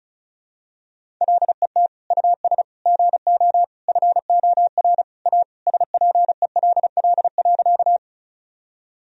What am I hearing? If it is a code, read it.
Morse code, heard as a keyed tone, about 36 wpm, 700 Hz. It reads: LET US GO FOR A SPELL.
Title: Sentences from Top 1000 Words - Review Part 1 - 35wpm